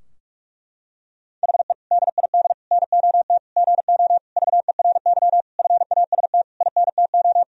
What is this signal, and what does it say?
Morse code, heard as a keyed tone, about 46 wpm, 700 Hz. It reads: HE DID NOT GO VERY FAST INTO
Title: Sentences from Top 1000 Words - Review Part 2 - 45wpm